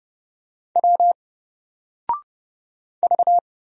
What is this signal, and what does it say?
Morse code, heard as a keyed tone, about 30 wpm, 700 Hz. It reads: W V
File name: Single-Letter-Number-30wpm